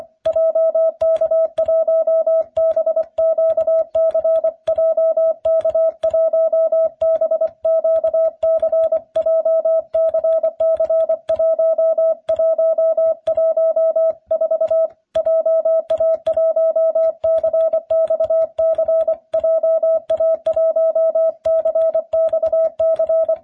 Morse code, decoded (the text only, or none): JK1BQC JK1BQC JCC1114 JA1CXC JA1CXC